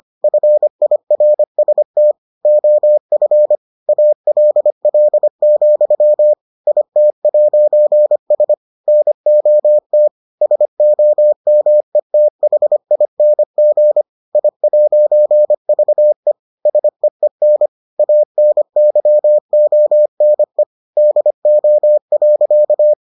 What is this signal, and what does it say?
Morse code, heard as a keyed tone, about 25 wpm, 600 Hz. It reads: FIRST OF ALL, IT'S NOT SOMETHING I'VE SEEN ANYONE DO.